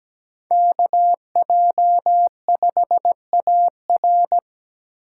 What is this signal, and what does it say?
Morse code, heard as a keyed tone, about 17 wpm, 700 Hz. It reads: KJ5AR
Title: USA Call Signs - 2x Character Spacing - 17wpm